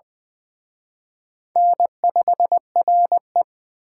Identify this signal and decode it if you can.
Morse code, heard as a keyed tone, about 23 wpm, 700 Hz. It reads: N5RE
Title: Speed Racing - USA 1x2 Call Signs 20wpm